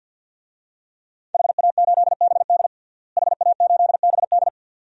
Morse code, heard as a keyed tone, about 52 wpm, 700 Hz. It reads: HA8BD HA8BD
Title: DXCC-100 Call Signs - 5 Characters - Repeated 2x - 50wpm